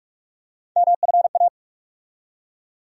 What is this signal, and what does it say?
Morse code, heard as a keyed tone, about 45 wpm, 700 Hz. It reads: MFA
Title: USA Call Sign Postfixes - 3 Letters 45wpm